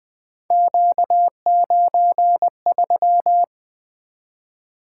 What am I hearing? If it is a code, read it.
Morse code, heard as a keyed tone, about 20 wpm, 700 Hz. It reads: Q93
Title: Three Letter-Number Combinations - 20wpm